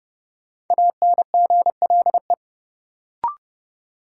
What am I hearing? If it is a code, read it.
Morse code, heard as a keyed tone, about 32 wpm, 700 Hz. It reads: ANGLE
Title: Sentences from Top 1500 Words - New Word Drill Alt - 30wpm